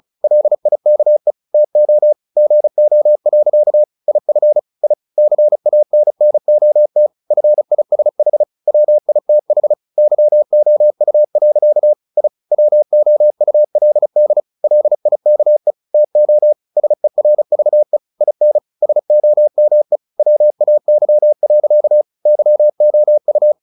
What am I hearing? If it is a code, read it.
Morse code, heard as a keyed tone, about 34 wpm, 600 Hz. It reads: LIKE TO GO. IF I CANNOT FISH WITH YOU. I WOULD LIKE TO SERVE IN SOME WAY. YOU
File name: the old man and the sea / the old man and the sea 1 35wpm